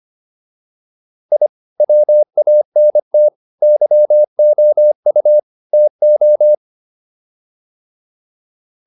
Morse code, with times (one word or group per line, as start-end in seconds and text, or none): 1.3-1.5 I
1.8-3.3 WANT
3.6-5.4 YOU
5.7-6.5 TO